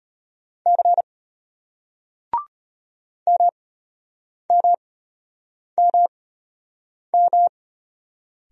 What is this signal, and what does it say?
Morse code, heard as a keyed tone, about 35 wpm, spaced as for 8 wpm, 700 Hz. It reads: C MMMM